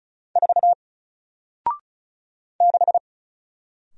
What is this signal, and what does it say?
Morse code, heard as a keyed tone, about 36 wpm, 700 Hz. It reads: 4 6
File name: Single-Letter-Number-35wpm